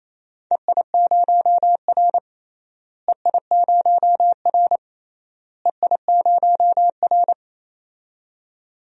Morse code, heard as a keyed tone, about 28 wpm, 700 Hz. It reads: EI0R EI0R EI0R
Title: DXCC-100 Call Signs - 4 Characters - Repeated 3x - 28wpm